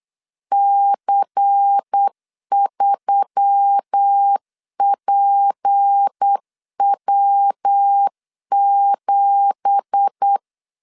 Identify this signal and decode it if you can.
Morse code, heard as a keyed tone, about 8 wpm, 800 Hz. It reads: C3PW7